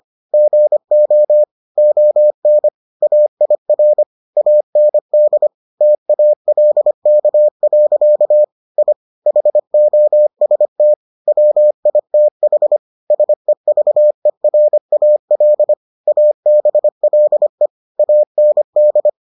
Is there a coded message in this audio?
Yes: Morse code from 0.3 to 19.1 s, heard as a keyed tone, about 25 wpm, 600 Hz. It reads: GO ON AIR AND TALK. I HOST WITH SEVERAL ABLE AND